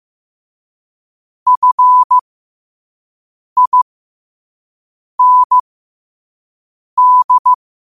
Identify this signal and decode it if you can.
Morse code, heard as a keyed tone, about 15 wpm, spaced as for 5 wpm, 1000 Hz. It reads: FIND
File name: qotdfiles / 5wpm